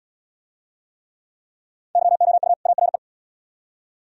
Morse code, heard as a keyed tone, about 74 wpm, 700 Hz. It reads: YOU ARE